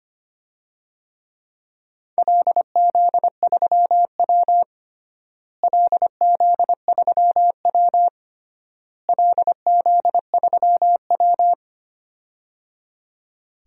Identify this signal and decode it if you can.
Morse code, heard as a keyed tone, about 25 wpm, 700 Hz. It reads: LZ3W LZ3W LZ3W
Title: DXCC-100 Call Signs - 4 Characters - Repeated 3x - 25wpm